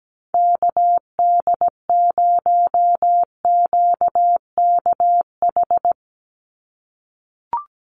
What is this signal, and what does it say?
Morse code, heard as a keyed tone, about 17 wpm, 700 Hz. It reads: KD0QKH
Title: USA Call Signs 17wpm